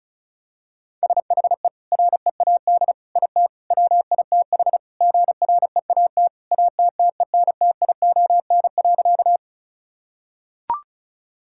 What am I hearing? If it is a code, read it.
Morse code, heard as a keyed tone, about 36 wpm, 700 Hz. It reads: SHE READ IT WITH GREAT ATTENTION.
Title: Sentences from Top 1000 Words - Review Part 6 - 35wpm